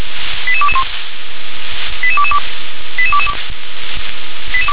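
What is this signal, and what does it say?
Five-tone selective call (ZVEI), heard as quick runs of five tones, tones of about 70 ms. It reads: 88221 88222 88222
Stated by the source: ZVEI five-tone call received by SDR.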